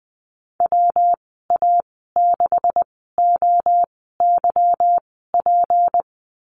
Morse code, heard as a keyed tone, about 20 wpm, spaced as for 15 wpm, 700 Hz. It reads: WA6OYP
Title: 238 Lesson 38g - Letter q - Call Signs 20wpm